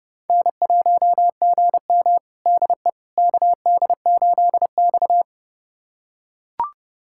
Morse code, heard as a keyed tone, about 30 wpm, 700 Hz. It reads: N1GM DE KD8X
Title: USA Callsign DE Callsign - 4 Characters - 30wpm